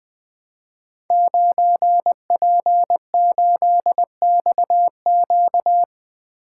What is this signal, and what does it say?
Morse code, heard as a keyed tone, about 21 wpm, 700 Hz. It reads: 9P8XQ
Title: Code Groups - 5 Letter:Numbers - 20wpm